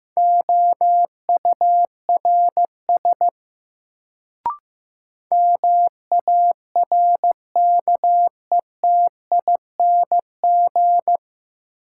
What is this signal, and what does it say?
Morse code, heard as a keyed tone, about 15 wpm, 700 Hz. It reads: OURS MARKETING
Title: Sentences from Top 1000 Words - Review Part 5 - New Word Drill - 15wpm